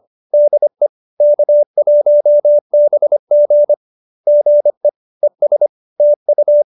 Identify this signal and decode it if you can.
Morse code, heard as a keyed tone, about 25 wpm, 590 Hz. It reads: DE K1BG GE ES TU